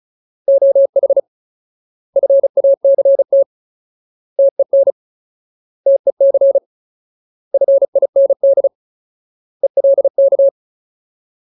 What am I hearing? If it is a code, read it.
Morse code, heard as a keyed tone, about 34 wpm, 550 Hz. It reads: OH FACT TEN TEC FIND ELK
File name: Ultimate Word List / 35WPM Ultimate